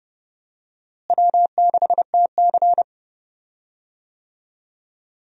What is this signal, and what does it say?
Morse code, heard as a keyed tone, about 30 wpm, 700 Hz. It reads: W6TC
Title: USA Call Signs - 4 Characters - Repeated 2x - 30wpm